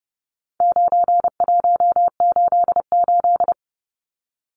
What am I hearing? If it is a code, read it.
Morse code, heard as a keyed tone, about 30 wpm, 700 Hz. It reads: 9188
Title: Numbers - 4-Digit 30wpm